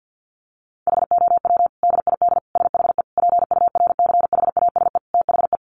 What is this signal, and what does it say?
Morse code, heard as a keyed tone, about 50 wpm, 700 Hz. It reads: HOW DID SHE PURCHASE THE
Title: Sentences from Top 800 Words 50wpm